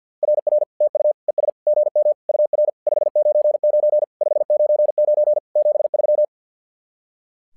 Morse code, heard as a keyed tone, about 50 wpm, 600 Hz. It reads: RR TU ES GM UR 599 599 73
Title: SOTA QSO Practice 50wpm